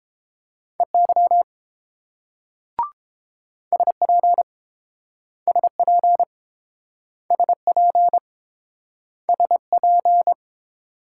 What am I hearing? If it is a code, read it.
Morse code, heard as a keyed tone, about 29 wpm, 700 Hz. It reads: EY SP SP SP SP